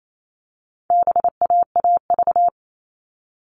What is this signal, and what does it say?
Morse code, heard as a keyed tone, about 28 wpm, 700 Hz. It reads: BAAV
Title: All Cut Numbers - 4-Digit 28wpm